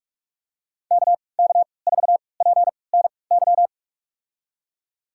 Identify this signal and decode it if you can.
Morse code, heard as a keyed tone, about 46 wpm, spaced as for 25 wpm, 700 Hz. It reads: KK4PNY
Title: USA Call Signs - 6 Characters - 3x Character Spacing - 45wpm